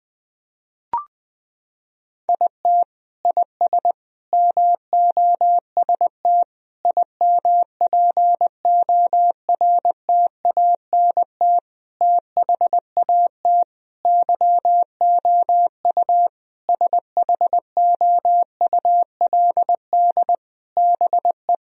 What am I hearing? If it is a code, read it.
Morse code, heard as a keyed tone, about 20 wpm, 700 Hz. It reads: IT IS MOST IMPORTANT THAT YOU SHOULD BE
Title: Sentences from Top 1000 Words - Review Part 1 - 20wpm